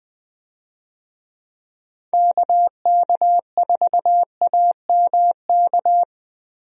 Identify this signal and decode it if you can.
Morse code, heard as a keyed tone, about 21 wpm, 700 Hz. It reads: KK4AMK
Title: USA Call Signs - 6 Characters - Repeated 3x - 20wpm